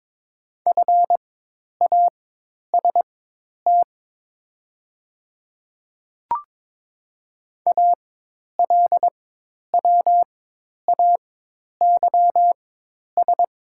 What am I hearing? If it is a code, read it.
Morse code, heard as a keyed tone, about 22 wpm, spaced as for 10 wpm, 700 Hz. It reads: FAST ALWAYS
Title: 4x Character Spacing - Top 500 Words - 22wpm